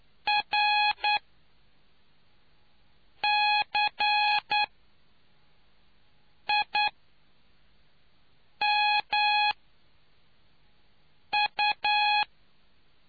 Morse code, heard as a keyed tone, about 9 wpm, spaced as for 4 wpm, 840 Hz. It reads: RCIMU